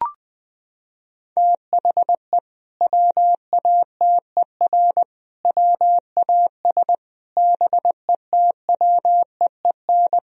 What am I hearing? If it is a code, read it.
Morse code, heard as a keyed tone, about 21 wpm, 700 Hz. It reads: THE WATER WAS BETWEEN
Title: Sentences from Top 1000 Words - Review Part 3 - 20wpm